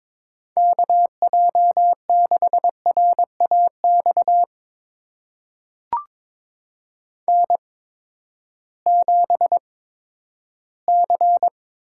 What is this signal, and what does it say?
Morse code, heard as a keyed tone, about 22 wpm, spaced as for 12 wpm, 700 Hz. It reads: KJ6RAX N7C
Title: USA Call Signs - 8x Character Spacing - 22wpm